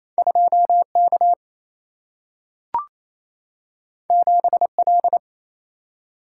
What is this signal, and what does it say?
Morse code, heard as a keyed tone, about 28 wpm, 700 Hz. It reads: 2K 7L
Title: Single-Letter Single-Number 28wpm